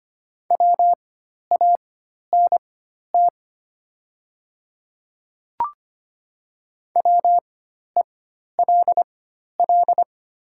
Morse code, heard as a keyed tone, about 25 wpm, spaced as for 12 wpm, 700 Hz. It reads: WANT WELL
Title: 4x Character Spacing - Top 200 Words - 25wpm